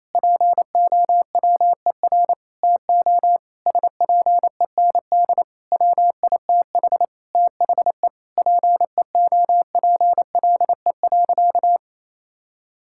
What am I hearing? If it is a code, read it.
Morse code, heard as a keyed tone, about 28 wpm, 700 Hz. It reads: POWER TO SPEND WITH THE PEOPLE.
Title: Sentences from Top 1000 Words - Review Part 5 - 28wpm